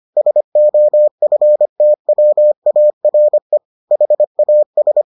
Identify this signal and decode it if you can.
Morse code, heard as a keyed tone, about 25 wpm, 600 Hz. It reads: SOFTWARE HAS